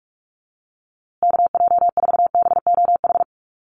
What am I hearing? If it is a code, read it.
Morse code, heard as a keyed tone, about 44 wpm, 700 Hz. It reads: KJ4BOH
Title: USA Call Signs Repeated 2x 45wpm